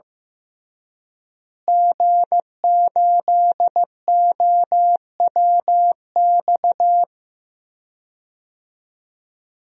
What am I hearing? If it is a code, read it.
Morse code, heard as a keyed tone, about 15 wpm, 700 Hz. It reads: G8OWX
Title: DXCC-100 Call Signs - 5 Characters - Repeated 3x - 15wpm